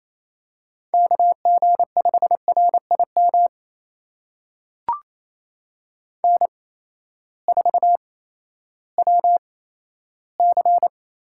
Morse code, heard as a keyed tone, about 28 wpm, spaced as for 14 wpm, 700 Hz. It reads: KG5RIM N4WC
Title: USA Call Signs - 8x Character Spacing - 28wpm